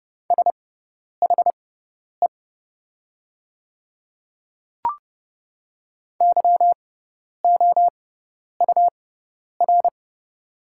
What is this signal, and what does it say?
Morse code, heard as a keyed tone, about 30 wpm, spaced as for 10 wpm, 690 Hz. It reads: SHE YOUR
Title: 6x Character Spacing - Top 100 Words - 30wpm